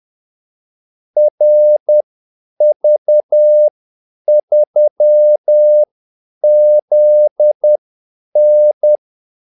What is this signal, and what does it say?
Morse code, heard as a keyed tone, about 10 wpm, spaced as for 8 wpm, 600 Hz. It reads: RV3ZN
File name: Callsigns-10-8wpm